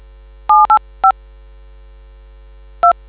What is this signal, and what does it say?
DTMF, heard as telephone touch tones, tones of about 120 ms. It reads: 785 2